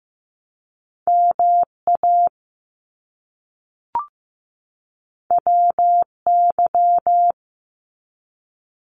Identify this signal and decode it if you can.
Morse code, heard as a keyed tone, about 15 wpm, 700 Hz. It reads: MA WY
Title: US States 15wpm